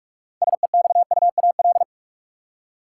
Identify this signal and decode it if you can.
Morse code, heard as a keyed tone, about 46 wpm, 700 Hz. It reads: SEXUAL
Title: Six Letter Words - 45wpm